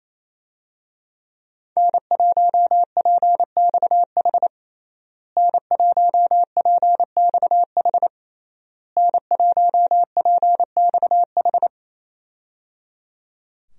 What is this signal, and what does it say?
Morse code, heard as a keyed tone, about 28 wpm, 700 Hz. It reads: N1PXH N1PXH N1PXH
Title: USA Call Signs - 5 Characters - Repeated 3x - 28wpm